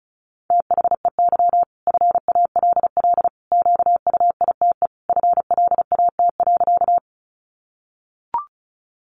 Morse code, heard as a keyed tone, about 34 wpm, 700 Hz. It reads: THEY FALL QUITE FLAT.
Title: Sentences from Top 1000 Words 35wpm